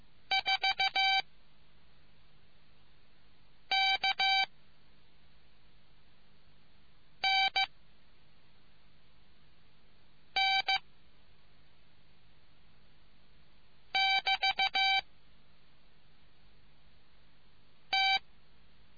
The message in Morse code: 4 K N N <BT> T

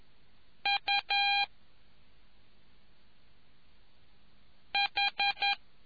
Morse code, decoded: UH

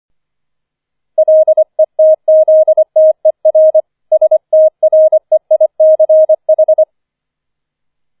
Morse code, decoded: LETZTER STREICH

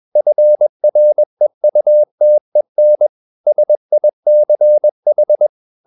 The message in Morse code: FREUTEN SICH

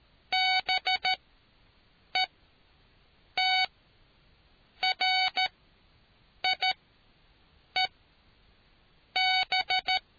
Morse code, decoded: BETRIEB

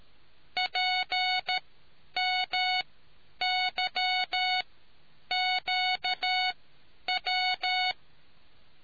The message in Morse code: PMYQW

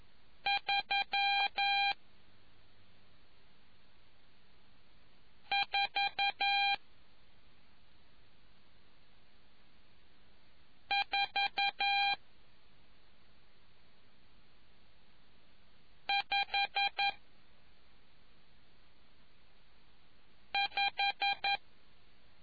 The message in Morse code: 34455

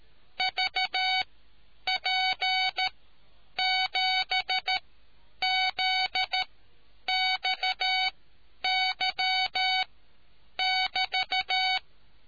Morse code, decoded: VP7ZXY<BT>

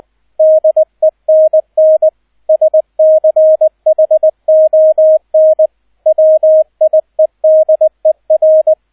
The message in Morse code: DENN SCHON WIEDER

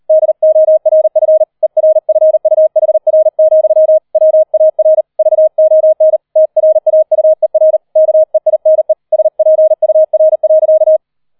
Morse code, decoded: DORF ERFUHR, WAR VON TRAUER KEINE SPUR.

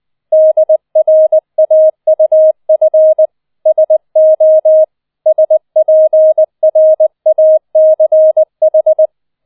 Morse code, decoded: DRAUF SO SPRACH

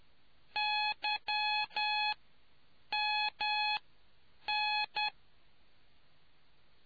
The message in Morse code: YMN

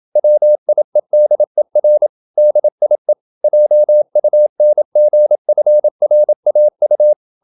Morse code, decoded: WIEDER DIE JUNGFRAU

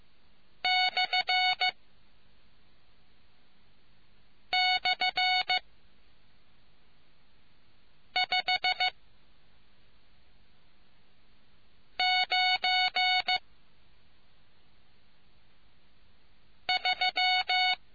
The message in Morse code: //593

